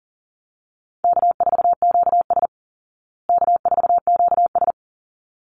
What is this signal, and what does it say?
Morse code, heard as a keyed tone, about 40 wpm, 700 Hz. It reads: K4QS K4QS
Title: Call Signs Repeated 3x - 40wpm